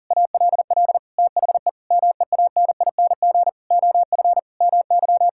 Morse code, heard as a keyed tone, about 37 wpm, 700 Hz. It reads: ALL THE MEANING OF MY
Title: Sentences from Top 1000 Words - Review Part 6 - 40wpm